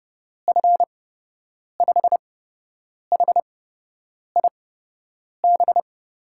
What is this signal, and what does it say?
Morse code, heard as a keyed tone, about 30 wpm, spaced as for 8 wpm, 700 Hz. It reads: F5HIB
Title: USA Call Signs - 6 Characters - 8x Character Spacing - 30wpm